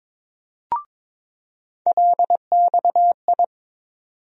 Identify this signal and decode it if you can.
Morse code, heard as a keyed tone, about 21 wpm, 700 Hz. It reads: LXI
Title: USA Call Sign Postfixes - 3 Letters 22wpm